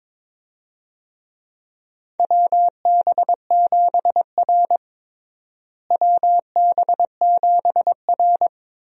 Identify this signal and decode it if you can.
Morse code, heard as a keyed tone, about 22 wpm, 700 Hz. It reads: WB7R WB7R
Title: USA Call Signs - 4 Characters - Repeated 2x - 22wpm